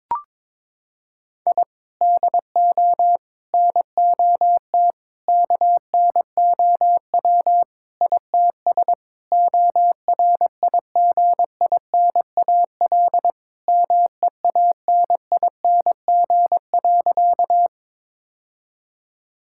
Morse code, heard as a keyed tone, about 22 wpm, 700 Hz. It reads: I DO NOT KNOW ITS ORIGINAL MEANING.
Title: Sentences from Top 1000 Words - Review Part 6 - 22wpm